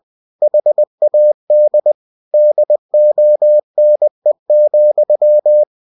Morse code, heard as a keyed tone, about 20 wpm, 600 Hz. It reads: HAD DONE,